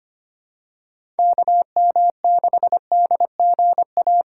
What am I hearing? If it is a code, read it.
Morse code, heard as a keyed tone, about 25 wpm, 700 Hz. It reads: KM6DGA
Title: USA Call Signs - 6 Characters - 3x Character Spacing - 25wpm